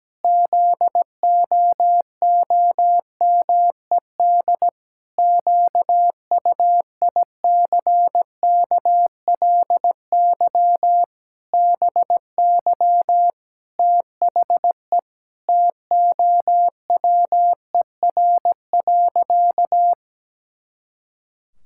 Morse code, heard as a keyed tone, about 17 wpm, 700 Hz. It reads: ZOOMED QUICKLY BY THE TOWER.
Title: Pangram Challenge v2 - 17wpm